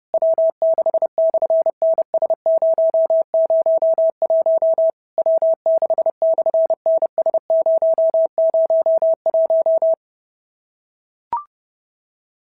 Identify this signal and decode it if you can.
Morse code, heard as a keyed tone, about 30 wpm, 650 Hz. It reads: W6/NS001 W6/NS001
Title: SOTA QSO Practice 30wpm